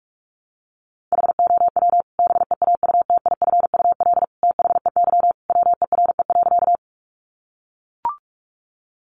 Morse code, heard as a keyed tone, about 44 wpm, 700 Hz. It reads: HOW BEAUTIFUL THEY WERE.
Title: Sentences from Top 900 Words 45wpm